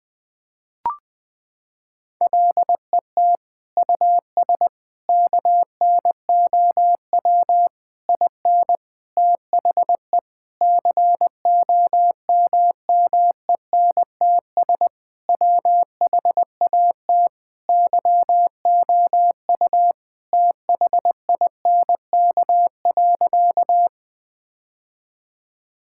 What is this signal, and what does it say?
Morse code, heard as a keyed tone, about 21 wpm, 700 Hz. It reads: LET US KNOW IN THE COMMENTS WHAT YOU THINK.
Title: Sentences with 6-Word Phrases - 20wpm